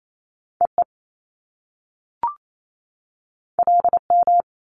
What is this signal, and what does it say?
Morse code, heard as a keyed tone, about 28 wpm, 700 Hz. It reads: EE LM
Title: Top Two Letter Combinations in Words 28wpm